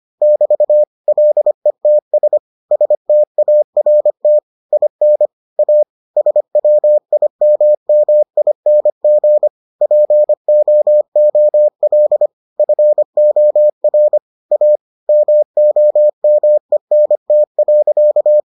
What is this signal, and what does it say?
Morse code, heard as a keyed tone, about 25 wpm, 600 Hz. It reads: <BT> LETS START IN A SWIMMING POOL FOR A MOMENT.